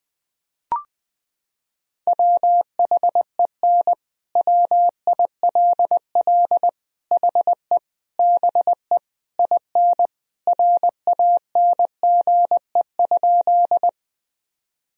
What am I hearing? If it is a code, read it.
Morse code, heard as a keyed tone, about 21 wpm, 700 Hz. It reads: WHEN WILL HE BE IN RANGE?
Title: Sentences from Top 1000 Words - Review Part 4 - 20wpm